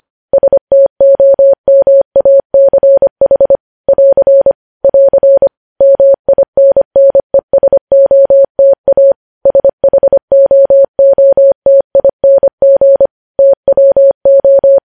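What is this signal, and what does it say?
Morse code, heard as a keyed tone, about 25 wpm, 550 Hz. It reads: STOMACH <AR> <AR> MINNESOTA SHOOTING TWO
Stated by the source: Morse practice sending news headlines at 25 wpm.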